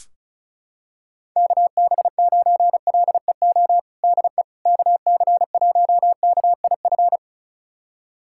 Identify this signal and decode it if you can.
Morse code, heard as a keyed tone, about 36 wpm, 690 Hz. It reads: KB9LEO DE KC1KIF
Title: USA Callsign DE Callsign - 6 Characters - 35wpm